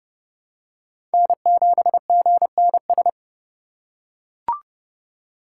Morse code, heard as a keyed tone, about 30 wpm, 700 Hz. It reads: N7GNS
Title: USA Call Signs - 4x Character Spacing - 30wpm